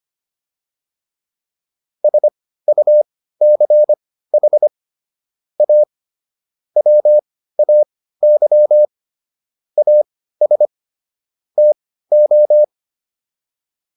Morse code, heard as a keyed tone, about 25 wpm, spaced as for 15 wpm, 600 Hz. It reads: SUCH A WAY AS TO